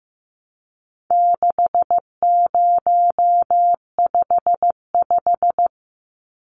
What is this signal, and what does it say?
Morse code, heard as a keyed tone, about 15 wpm, 700 Hz. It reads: 6055
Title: Numbers - 4-Digit 15wpm